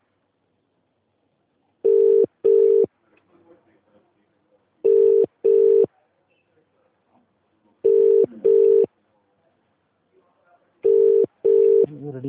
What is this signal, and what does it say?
Ringback tone (UK), 400 + 450 Hz, with speech.